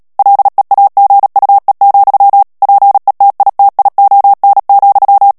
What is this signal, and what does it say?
Morse code, heard as a keyed tone, about 37 wpm, 800 Hz. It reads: LEAGUE, PETITION,
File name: morse mp3 / ANGLAIS texte clair 3 vitesse 35 mots minute